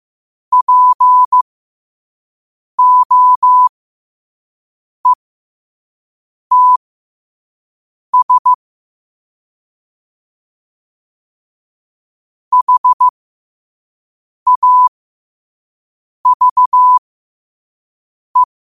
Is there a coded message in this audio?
Yes: Morse code from 0.5 to 18.4 s, heard as a keyed tone, about 15 wpm, spaced as for 5 wpm, 1000 Hz. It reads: POETS HAVE